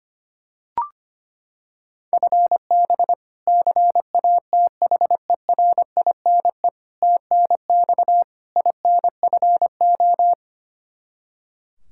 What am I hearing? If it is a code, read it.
Morse code, heard as a keyed tone, about 25 wpm, 700 Hz. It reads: FB CATHERINE TNX INFO
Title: K1USN SST QSO Practice 25wpm